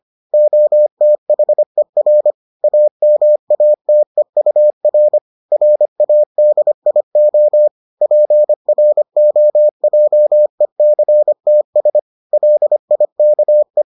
Morse code, heard as a keyed tone, about 25 wpm, 600 Hz. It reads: OTHER AMATEUR RADIO PROJECTS LIKE